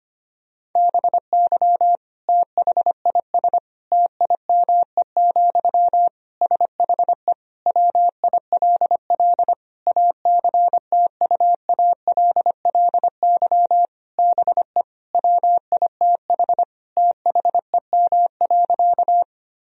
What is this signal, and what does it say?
Morse code, heard as a keyed tone, about 25 wpm, 700 Hz. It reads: BY THIS TIME, SHE WILL ACTUALLY BE WITH THEM.
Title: Sentences from Top 1000 Words - Review Part 4 - 25wpm